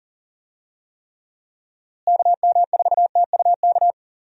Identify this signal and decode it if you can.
Morse code, heard as a keyed tone, about 40 wpm, 700 Hz. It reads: KM4TUK
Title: USA Call Signs - 6 Characters - Repeated 3x - 40wpm